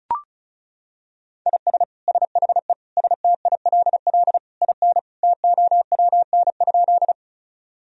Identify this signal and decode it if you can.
Morse code, heard as a keyed tone, about 34 wpm, 700 Hz. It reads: IS SHE STILL IN TOWN?
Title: Sentences from Top 1000 Words - Review Part 2 - 35wpm